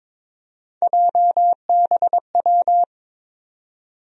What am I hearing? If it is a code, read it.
Morse code, heard as a keyed tone, about 22 wpm, 700 Hz. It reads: JBW
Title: USA Call Sign Postfixes - 3 Letters 22wpm